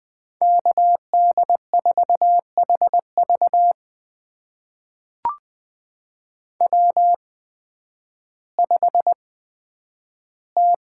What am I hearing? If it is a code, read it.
Morse code, heard as a keyed tone, about 20 wpm, spaced as for 11 wpm, 700 Hz. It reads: KD4HV W5T
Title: USA Call Signs - 5 Characters - 8x Character Spacing - 20wpm